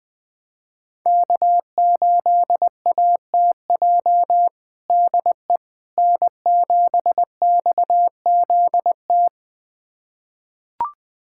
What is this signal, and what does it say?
Morse code, heard as a keyed tone, about 21 wpm, 700 Hz. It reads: K8ATJ DE N7XZT
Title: USA Callsign DE Callsign - 5 Characters - 20wpm